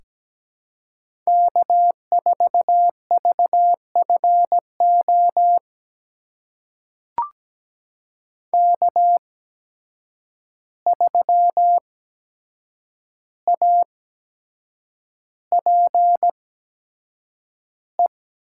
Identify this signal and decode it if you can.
Morse code, heard as a keyed tone, about 17 wpm, spaced as for 7 wpm, 700 Hz. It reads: K4VFO K3APE
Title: USA Call Signs - 5 Characters - 8x Character Spacing - 17wpm